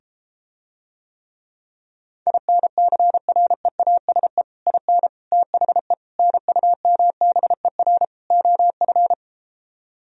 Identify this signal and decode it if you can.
Morse code, heard as a keyed tone, about 33 wpm, 700 Hz. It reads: INCREASE IN THE NUMBER OF